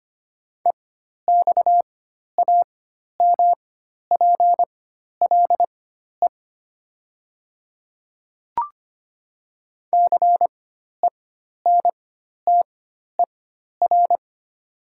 Morse code, heard as a keyed tone, about 25 wpm, spaced as for 12 wpm, 700 Hz. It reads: EXAMPLE CENTER
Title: 4x Character Spacing - Top 500 Words - 25wpm